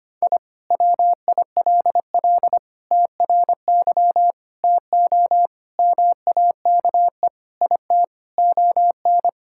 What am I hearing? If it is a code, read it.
Morse code, heard as a keyed tone, about 25 wpm, 700 Hz. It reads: I WILL TRY TO MAKE IT ON
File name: practice / 25wpm / Sentences with 3-Word Phrases - 25wpm